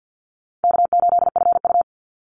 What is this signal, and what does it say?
Morse code, heard as a keyed tone, about 48 wpm, 700 Hz. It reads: K8FU
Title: Easy Call Signs - 50wpm